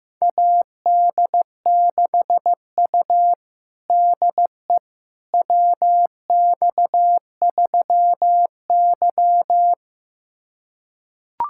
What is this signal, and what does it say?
Morse code, heard as a keyed tone, about 16 wpm, 700 Hz. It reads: AD6U DE WX3Y
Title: USA Callsign DE Callsign - 4 Characters - 15wpm